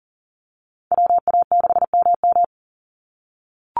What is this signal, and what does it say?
Morse code, heard as a keyed tone, about 40 wpm, 700 Hz. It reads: WA6MM
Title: NA SOTA Activator Callsigns 40wpm